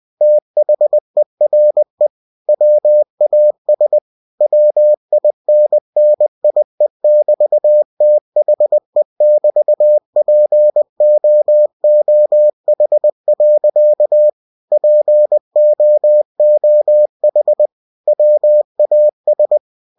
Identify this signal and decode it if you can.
Morse code, heard as a keyed tone, about 20 wpm, 600 Hz. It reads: THERE WAS WINNIE<BT>THE<BT>POOH. POOH WAS
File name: cw-books / winnie-the-pooh / winnie-20wpm-page-0021